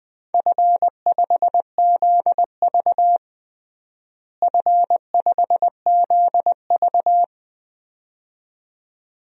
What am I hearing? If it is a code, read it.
Morse code, heard as a keyed tone, about 20 wpm, 700 Hz. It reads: F5ZV F5ZV
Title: DXCC-100 Call Signs - 4 Characters - Repeated 2x - 20wpm